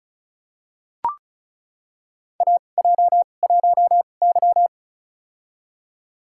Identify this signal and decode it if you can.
Morse code, heard as a keyed tone, about 36 wpm, spaced as for 26 wpm, 690 Hz. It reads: AJ1Y
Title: USA Call Signs - 4 Characters - 2x Character Spacing - 35wpm